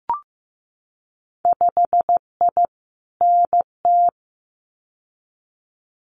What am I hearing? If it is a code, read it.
Morse code, heard as a keyed tone, about 15 wpm, 700 Hz. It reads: 5I NT
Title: Winter Field Day Exchanges Encore 15wpm